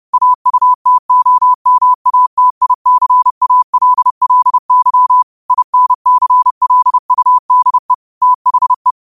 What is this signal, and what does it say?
Morse code, heard as a keyed tone, about 30 wpm, 1000 Hz. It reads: AUTOMATICALLY INCLUDE THE